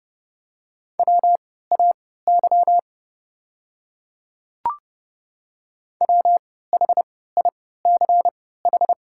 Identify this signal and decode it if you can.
Morse code, heard as a keyed tone, about 30 wpm, spaced as for 17 wpm, 700 Hz. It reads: WAY WHICH
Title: 3x Character Spacing - Top 100 Words - 30wpm